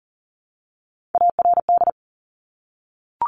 Morse code, heard as a keyed tone, about 38 wpm, 700 Hz. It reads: ARD
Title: Top Three Letter Combinations in Words 40wpm